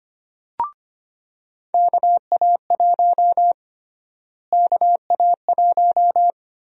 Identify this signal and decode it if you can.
Morse code, heard as a keyed tone, about 25 wpm, 700 Hz. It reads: KA1 KA1
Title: USA Call Sign Prefixes - Repeated 2x - 25wpm